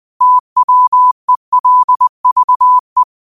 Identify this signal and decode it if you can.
Morse code, heard as a keyed tone, about 20 wpm, 1000 Hz. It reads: TWELVE